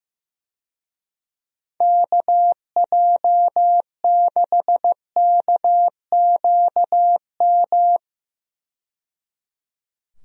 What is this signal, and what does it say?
Morse code, heard as a keyed tone, about 15 wpm, 700 Hz. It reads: KJ6KQM
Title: USA Call Signs - 6 Characters - 15wpm